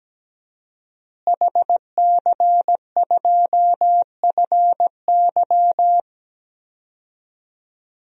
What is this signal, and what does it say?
Morse code, heard as a keyed tone, about 17 wpm, 700 Hz. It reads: HC2FY
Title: DXCC-100 Call Signs - 5 Characters - Repeated 2x - 17wpm